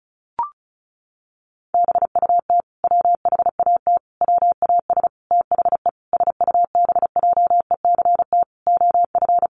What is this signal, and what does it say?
Morse code, heard as a keyed tone, about 34 wpm, 700 Hz. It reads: BUT WHAT WAS THE SUBJECT OF
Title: Sentences from Top 1100 Words 35wpm